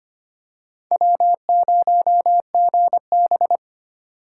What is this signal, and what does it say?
Morse code, heard as a keyed tone, about 25 wpm, 700 Hz. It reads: W0GB
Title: USA Call Signs - 4x Character Spacing - 25wpm